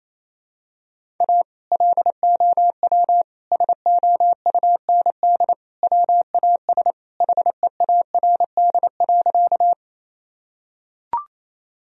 Morse code, heard as a keyed tone, about 28 wpm, 700 Hz. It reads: A LOW SOUND WAS HEARD.
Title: Sentences from Top 1000 Words - Review Part 1 - 28wpm